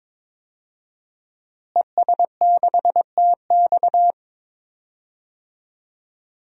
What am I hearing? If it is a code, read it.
Morse code, heard as a keyed tone, about 22 wpm, 700 Hz. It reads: ES6TX
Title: DXCC-100 Call Signs - 5 Characters - Repeated 2x - 22wpm